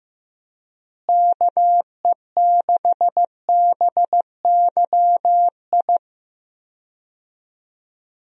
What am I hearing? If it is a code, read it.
Morse code, heard as a keyed tone, about 15 wpm, 700 Hz. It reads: KE6BYI
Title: USA Call Signs - 6 Characters - Repeated 3x - 15wpm